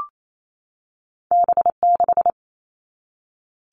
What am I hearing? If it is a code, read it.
Morse code, heard as a keyed tone, about 28 wpm, 700 Hz. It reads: B6
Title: All Cut Numbers - 2-Digit 28wpm